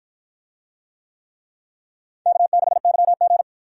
Morse code, heard as a keyed tone, about 53 wpm, 700 Hz. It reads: K6YG